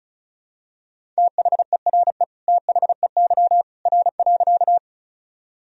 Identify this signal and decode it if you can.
Morse code, heard as a keyed tone, about 34 wpm, 700 Hz. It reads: THERE THEY R.
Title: Sentences from Top 100 Words and Abbreviations 35wpm